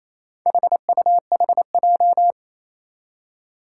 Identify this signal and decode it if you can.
Morse code, heard as a keyed tone, about 28 wpm, 700 Hz. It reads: HUHJ
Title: Four Challenging Letter-Number Combinations 28wpm